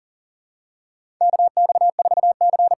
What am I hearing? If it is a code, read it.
Morse code, heard as a keyed tone, about 40 wpm, 700 Hz. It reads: KX4C
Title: USA Call Signs - 5 Characters - 8x Character Spacing - 40wpm